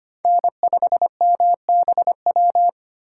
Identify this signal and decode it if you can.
Morse code, heard as a keyed tone, about 25 wpm, 700 Hz. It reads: N5MBW
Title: USA Call Signs - 5 Characters - 6x Character Spacing - 25wpm